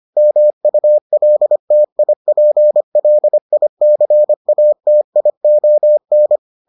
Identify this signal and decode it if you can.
Morse code, heard as a keyed tone, about 25 wpm, 600 Hz. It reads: MULTIPLICATION